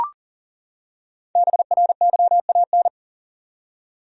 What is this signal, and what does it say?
Morse code, heard as a keyed tone, about 40 wpm, 700 Hz. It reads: BRYAN
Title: USA Common Names - Max 5 Letters - 40wpm